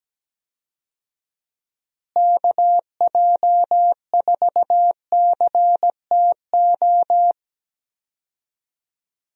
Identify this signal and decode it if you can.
Morse code, heard as a keyed tone, about 17 wpm, 700 Hz. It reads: KJ4CTO
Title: USA Call Signs - 6 Characters - Repeated 2x - 17wpm